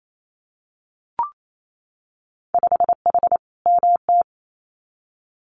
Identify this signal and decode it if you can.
Morse code, heard as a keyed tone, about 28 wpm, 700 Hz. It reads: 5H MT
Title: Winter Field Day Exchanges Encore 28wpm